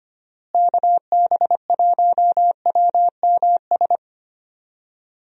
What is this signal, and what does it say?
Morse code, heard as a keyed tone, about 25 wpm, 700 Hz. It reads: KB1WMS
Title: USA Call Signs - 6 Characters - Repeated 2x - 25wpm